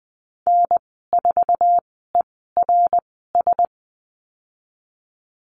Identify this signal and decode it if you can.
Morse code, heard as a keyed tone, about 20 wpm, spaced as for 15 wpm, 700 Hz. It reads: N4ERS
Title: 088 Lesson 15g - Number 5 - Call Signs 20wpm